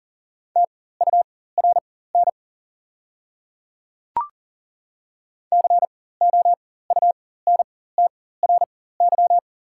Morse code, heard as a keyed tone, about 40 wpm, spaced as for 19 wpm, 700 Hz. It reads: TURN COUNTRY
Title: 4x Character Spacing - Top 300 Words - 40wpm